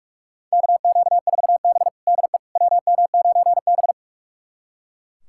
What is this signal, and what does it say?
Morse code, heard as a keyed tone, about 42 wpm, 700 Hz. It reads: KQ4B DE WM9B
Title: USA Callsign DE Callsign - 4 Characters - 45wpm